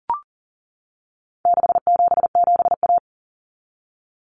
Morse code, heard as a keyed tone, about 40 wpm, 700 Hz. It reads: 677A
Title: Common Cut Numbers - 4-Digit 40wpm